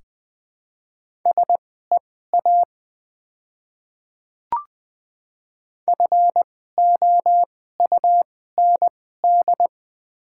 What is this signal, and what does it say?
Morse code, heard as a keyed tone, about 20 wpm, spaced as for 15 wpm, 700 Hz. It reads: SEA FOUND
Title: 2x Character Spacing - Top 300 Words - 20wpm